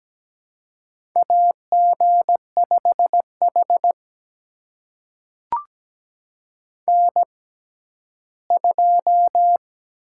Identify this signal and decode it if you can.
Morse code, heard as a keyed tone, about 17 wpm, spaced as for 12 wpm, 700 Hz. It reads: AG5H N2
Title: USA Call Signs - 6x Character Spacing - 17wpm